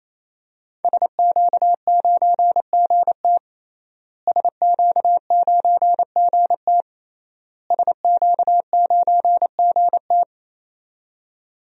Morse code, heard as a keyed tone, about 28 wpm, 700 Hz. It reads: SQ9GT SQ9GT SQ9GT
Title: DXCC-100 Call Signs - 5 Characters - Repeated 3x - 28wpm